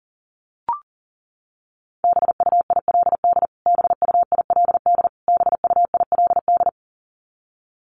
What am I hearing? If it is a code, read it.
Morse code, heard as a keyed tone, about 37 wpm, 700 Hz. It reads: BUILD BUILD BUILD
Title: Top 500 Words Repeated 40wpm